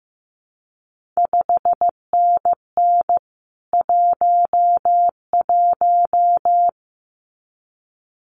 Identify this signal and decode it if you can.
Morse code, heard as a keyed tone, about 15 wpm, 700 Hz. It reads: 5NN 11
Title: CQ WW DX Contest Exchange Encore - 15wpm